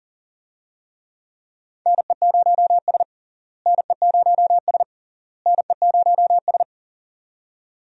Morse code, heard as a keyed tone, about 40 wpm, 700 Hz. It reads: NE0S NE0S NE0S
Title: USA Call Signs - 4 Characters - Repeated 3x - 40wpm